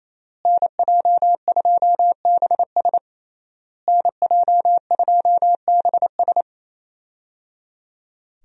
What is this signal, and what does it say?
Morse code, heard as a keyed tone, about 27 wpm, 700 Hz. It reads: NJ2BS NJ2BS
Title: USA Call Signs - 5 Characters - Repeated 3x - 28wpm